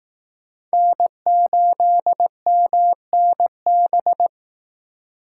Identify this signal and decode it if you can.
Morse code, heard as a keyed tone, about 18 wpm, 700 Hz. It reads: N8MNB
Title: Speed Racing - USA 1x3 Call Signs 15wpm